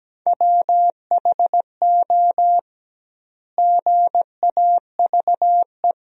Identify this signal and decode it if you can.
Morse code, heard as a keyed tone, about 17 wpm, 700 Hz. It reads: WHO GAVE
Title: 2x Word-Spacing - Sentences from Top 500 Words - 17wpm